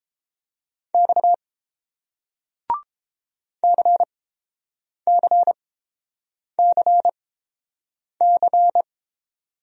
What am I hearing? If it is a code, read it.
Morse code, heard as a keyed tone, about 30 wpm, spaced as for 7 wpm, 700 Hz. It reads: X CCCC